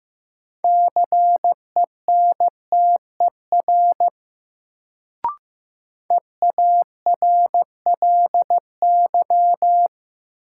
Morse code, heard as a keyed tone, about 15 wpm, 700 Hz. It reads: CENTER EARLY
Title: Sets of 3 Words - New Word Drill - 15wpm